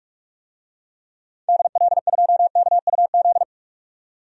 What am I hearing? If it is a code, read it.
Morse code, heard as a keyed tone, about 42 wpm, 700 Hz. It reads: DL2KUZ
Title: DXCC-100 Call Signs - 6 Characters - 45wpm